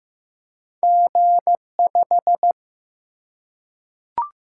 Morse code, heard as a keyed tone, about 15 wpm, 700 Hz. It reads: G5
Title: Single-Letter Single-Number 15wpm